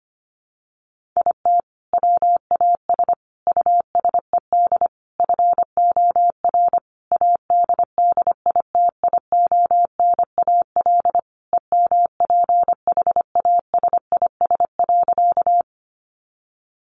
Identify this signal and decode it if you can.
Morse code, heard as a keyed tone, about 25 wpm, 700 Hz. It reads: IT WAS USED FOR ADDITIONAL EMPHASIS.
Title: Sentences from Top 1400 Words 25wpm